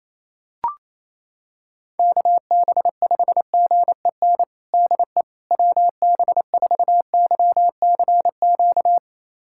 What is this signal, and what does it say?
Morse code, heard as a keyed tone, about 27 wpm, 700 Hz. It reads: KB5GEN DE WB4YCQ
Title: USA Callsign DE Callsign - 6 Characters - 28wpm